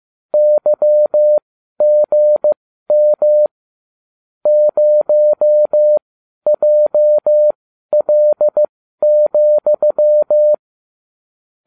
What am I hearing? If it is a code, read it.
Morse code, heard as a keyed tone, about 15 wpm, spaced as for 12 wpm, 600 Hz. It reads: YGM 0JL,